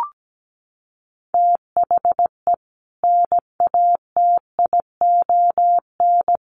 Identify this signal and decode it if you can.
Morse code, heard as a keyed tone, about 17 wpm, 700 Hz. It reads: THE NATION
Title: Sets of 5 Words 17wpm